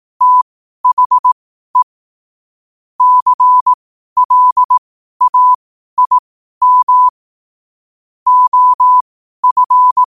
Morse code, heard as a keyed tone, about 18 wpm, spaced as for 12 wpm, 1000 Hz. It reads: THE CLAIM OF
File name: qotdfiles / 12wpm